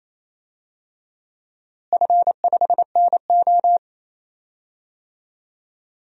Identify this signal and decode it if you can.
Morse code, heard as a keyed tone, about 28 wpm, 700 Hz. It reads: F5NO
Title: DXCC-100 Call Signs - 4 Characters - Repeated 2x - 28wpm